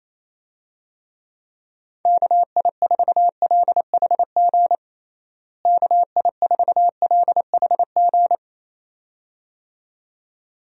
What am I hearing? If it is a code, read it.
Morse code, heard as a keyed tone, about 28 wpm, 700 Hz. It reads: KI4LHG KI4LHG
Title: USA Call Signs - 6 Characters - Repeated 2x - 28wpm